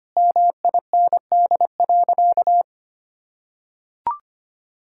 Morse code, heard as a keyed tone, about 25 wpm, 700 Hz. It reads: MIND.
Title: Sentences from Top 1000 Words - Review Part 6 - 25wpm